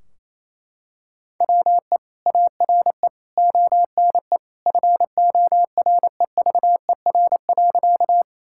Morse code, heard as a keyed tone, about 28 wpm, 700 Hz. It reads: WE ARE ONE FOREVER.
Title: Sentences from Top 1600 Words 28wpm